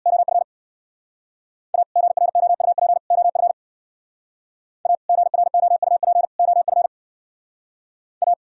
Morse code, heard as a keyed tone, about 63 wpm, 700 Hz. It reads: OF A GROUP OF A GROUP OF A